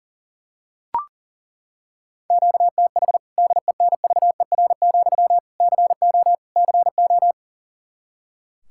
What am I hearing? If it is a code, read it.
Morse code, heard as a keyed tone, about 40 wpm, 700 Hz. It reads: QTH DENVER, CO CO
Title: K1USN SST QSO Practice 40wpm